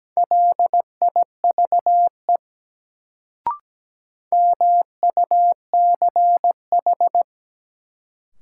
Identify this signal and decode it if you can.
Morse code, heard as a keyed tone, about 17 wpm, 700 Hz. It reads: LIVE MUCH
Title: Sentences from Top 200 Words - New Word Drill - 17wpm